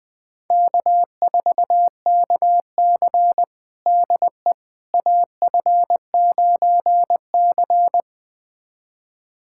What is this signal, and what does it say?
Morse code, heard as a keyed tone, about 20 wpm, 700 Hz. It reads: K4KC DE AF9C
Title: USA Callsign DE Callsign - 4 Characters - 20wpm